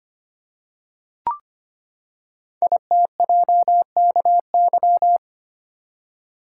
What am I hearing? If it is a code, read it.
Morse code, heard as a keyed tone, about 25 wpm, 700 Hz. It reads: ITJKY
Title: Code Groups - 5 Letters - 25wpm